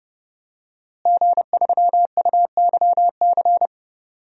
Morse code, heard as a keyed tone, about 30 wpm, 700 Hz. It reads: G3UYC
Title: DXCC-100 Call Signs - 5 Characters - Repeated 3x - 30wpm